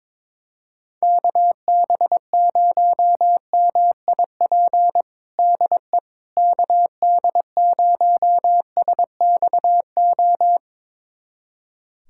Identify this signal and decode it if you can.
Morse code, heard as a keyed tone, about 21 wpm, 700 Hz. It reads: KB0MIP DE KD0SXO
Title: USA Callsign DE Callsign 22wpm